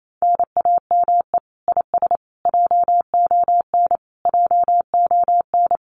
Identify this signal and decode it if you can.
Morse code, heard as a keyed tone, about 28 wpm, 700 Hz. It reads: NAME IS JON JON
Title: Common QSO Phrases Encore 28wpm